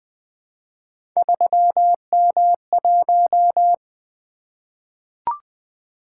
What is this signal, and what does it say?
Morse code, heard as a keyed tone, about 20 wpm, 700 Hz. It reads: 3M1
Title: Three Letter-Number Combinations - 20wpm